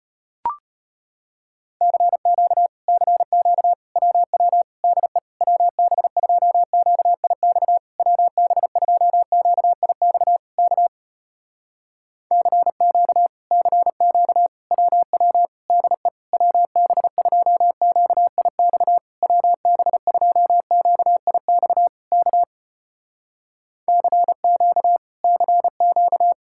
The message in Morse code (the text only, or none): CQ CQ WW DE WB2QIX WB2QIX K CQ CQ WW DE WB2QIX WB2QIX K CQ CQ